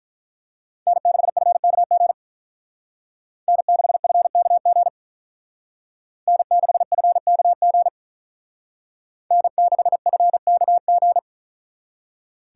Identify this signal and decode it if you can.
Morse code, heard as a keyed tone, about 43 wpm, 700 Hz. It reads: N6FKG N6FKG N6FKG N6FKG
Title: Speed Racing - USA 1x3 Call Signs 35wpm